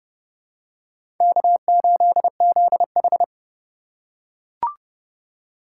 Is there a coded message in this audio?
Yes: Morse code from 1.2 to 3.2 s, heard as a keyed tone, about 29 wpm, 700 Hz. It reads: K8ZH